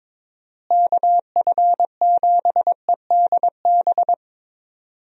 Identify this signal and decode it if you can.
Morse code, heard as a keyed tone, about 22 wpm, 700 Hz. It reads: KF7EDB
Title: USA Call Signs - 6 Characters - Repeated 2x - 22wpm